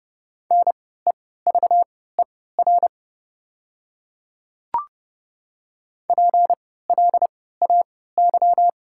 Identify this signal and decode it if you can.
Morse code, heard as a keyed tone, about 29 wpm, spaced as for 17 wpm, 690 Hz. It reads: NEVER PLAY
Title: 3x Character Spacing - Top 300 Words - 30wpm